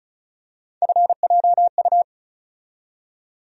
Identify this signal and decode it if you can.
Morse code, heard as a keyed tone, about 34 wpm, 700 Hz. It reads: FJU
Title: USA Call Sign Postfixes - 3 Letters 35wpm